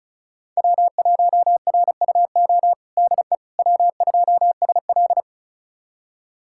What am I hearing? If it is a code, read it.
Morse code, heard as a keyed tone, about 36 wpm, 700 Hz. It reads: W1RUO DE W2SL
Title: USA Callsign DE Callsign 35wpm